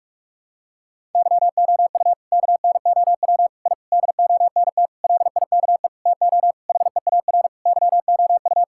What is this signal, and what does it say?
Morse code, heard as a keyed tone, about 46 wpm, 700 Hz. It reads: YOU KNOW I DONT LIKE TO HEAR YOU
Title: Sentences from Top 1000 Words - Review Part 6 - 45wpm